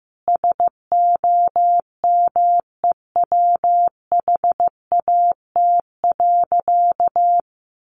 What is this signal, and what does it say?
Morse code, heard as a keyed tone, about 15 wpm, 700 Hz. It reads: SOMEWHAT.
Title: Sentences from Top 1300 Words 15wpm